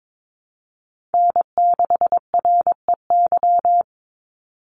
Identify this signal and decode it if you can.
Morse code, heard as a keyed tone, about 22 wpm, 700 Hz. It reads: N6REY
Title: USA Call Signs 22wpm